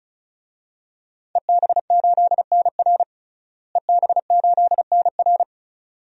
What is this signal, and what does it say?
Morse code, heard as a keyed tone, about 36 wpm, 700 Hz. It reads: EB8NR EB8NR
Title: DXCC-100 Call Signs - 5 Characters - Repeated 2x - 35wpm